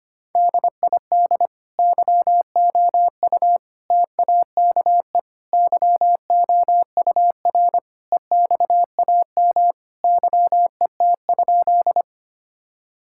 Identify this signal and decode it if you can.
Morse code, heard as a keyed tone, about 25 wpm, 700 Hz. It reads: DID YOU TAKE YOUR EXAM YET?
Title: Sentences from Top 1000 Words - Review Part 5 - 25wpm